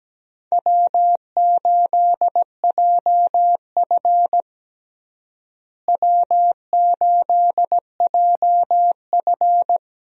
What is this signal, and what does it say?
Morse code, heard as a keyed tone, about 17 wpm, 700 Hz. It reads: W8JF W8JF
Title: USA Call Signs - 4 Characters - Repeated 2x - 17wpm